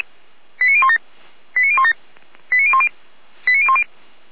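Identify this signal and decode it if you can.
Five-tone selective call (ZVEI), heard as quick runs of five tones, tones of about 70 ms. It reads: 89017 89017 89010 89010
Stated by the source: ZVEI five-tone selective call received by SDR.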